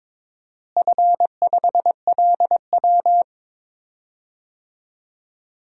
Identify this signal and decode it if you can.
Morse code, heard as a keyed tone, about 22 wpm, 700 Hz. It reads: F5LW
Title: DXCC-100 Call Signs - 4 Characters - Repeated 3x - 22wpm